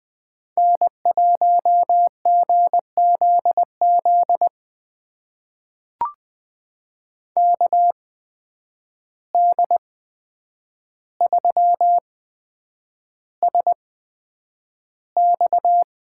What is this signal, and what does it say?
Morse code, heard as a keyed tone, about 20 wpm, spaced as for 9 wpm, 700 Hz. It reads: N1GZZ KD3SX
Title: USA Call Signs - 5 Characters - 8x Character Spacing - 20wpm